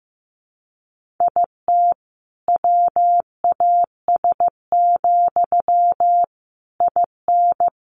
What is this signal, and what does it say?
Morse code, heard as a keyed tone, about 15 wpm, 700 Hz. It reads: IT WAS, IN
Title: Sentences from Top 1400 Words 15wpm